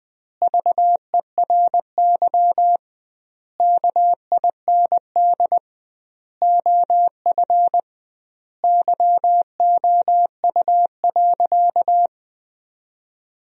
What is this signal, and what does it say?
Morse code, heard as a keyed tone, about 20 wpm, 700 Hz. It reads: VERY KIND OF YOU.
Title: 2x Word-Spacing - Sentences from Top 200 Words - 20wpm